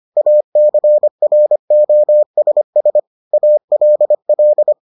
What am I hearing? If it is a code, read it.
Morse code, heard as a keyed tone, about 25 wpm, 600 Hz. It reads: ACROSS ALL